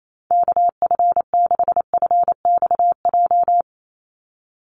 Morse code, heard as a keyed tone, about 28 wpm, 700 Hz. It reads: KF6FXJ
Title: USA Call Signs 28wpm